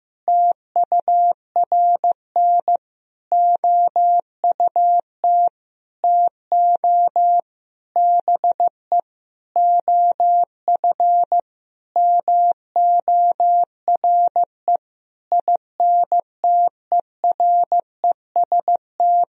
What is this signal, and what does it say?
Morse code, heard as a keyed tone, about 15 wpm, 700 Hz. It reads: TURN OUT TO BE OF MORE INTEREST
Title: Sentences from Top 400 Words and Abbreviations 15wpm